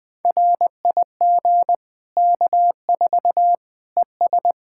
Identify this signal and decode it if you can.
Morse code, heard as a keyed tone, about 20 wpm, 700 Hz. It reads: RIG K4 ES
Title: Practice QSO - Exchange 2 - Info - 20wpm